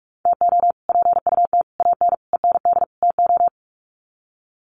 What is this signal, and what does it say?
Morse code, heard as a keyed tone, about 44 wpm, 700 Hz. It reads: TO PUT AN END TO
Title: Sets of 5 Words Encore2 45wpm